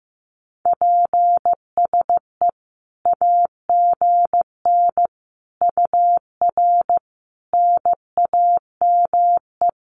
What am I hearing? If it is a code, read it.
Morse code, heard as a keyed tone, about 15 wpm, 700 Hz. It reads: PSE AGN UR NAME
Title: Common QSO Phrases 15wpm